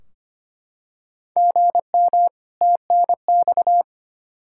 Morse code, heard as a keyed tone, about 25 wpm, 700 Hz. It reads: GM TNX
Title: Practice QSO - Exchange 1 - Beginning - 25wpm